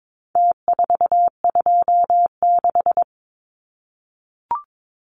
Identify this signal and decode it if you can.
Morse code, heard as a keyed tone, about 22 wpm, 700 Hz. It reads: T426
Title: Numbers - Contest Style - 4-Digit 22wpm